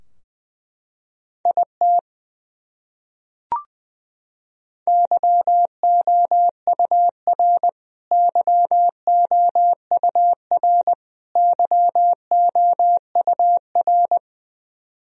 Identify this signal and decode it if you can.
Morse code, heard as a keyed tone, about 20 wpm, 700 Hz. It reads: IT YOUR YOUR YOUR
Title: Top 100 Words Repeated 20wpm